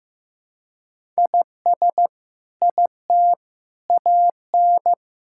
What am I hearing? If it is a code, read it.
Morse code, heard as a keyed tone, about 15 wpm, 700 Hz. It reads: IS IT AN
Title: Sentences from Top 1000 Words - Review Part 6 - 15wpm